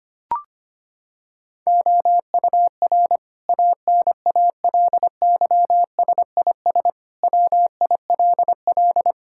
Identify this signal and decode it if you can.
Morse code, heard as a keyed tone, about 25 wpm, 700 Hz. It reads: OUR ANALYSIS WILL
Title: Sentences from Top 1000 Words - Review Part 5 - 25wpm